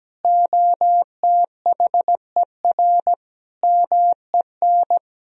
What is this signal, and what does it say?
Morse code, heard as a keyed tone, about 17 wpm, 700 Hz. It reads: OTHER MEN
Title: Sentences from Top 1000 Words - Review Part 2 - 17wpm